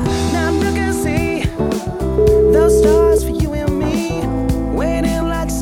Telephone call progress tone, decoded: ringback tone

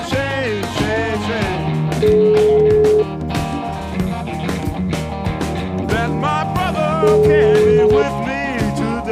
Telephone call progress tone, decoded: ringback tone

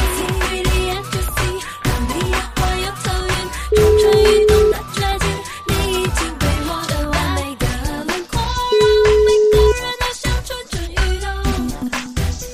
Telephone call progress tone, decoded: ringback tone

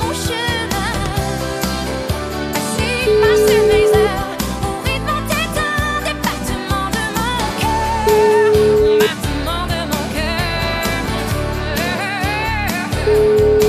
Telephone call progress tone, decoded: ringback tone

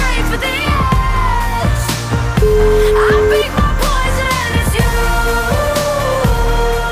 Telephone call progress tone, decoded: ringback tone

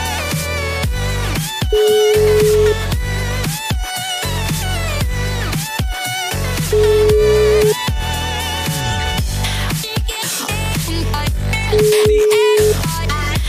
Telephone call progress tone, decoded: ringback tone

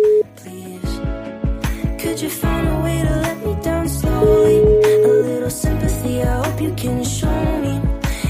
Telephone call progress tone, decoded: ringback tone

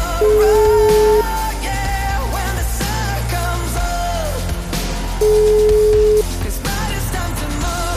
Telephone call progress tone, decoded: ringback tone

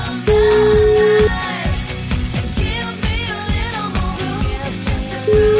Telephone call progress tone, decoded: ringback tone

